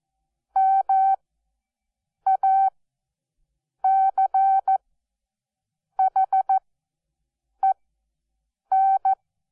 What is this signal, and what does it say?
Morse code, heard as a keyed tone, about 14 wpm, spaced as for 6 wpm, 770 Hz. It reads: MACHEN
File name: Onlinelehrgang / morsen / DJ4UF Morsen12